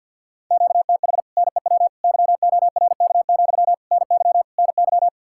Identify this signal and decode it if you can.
Morse code, heard as a keyed tone, about 62 wpm, 700 Hz. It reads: QTH NEW YORK, NY NY
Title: Speed-Racing Common QSO Phrases - 50wpm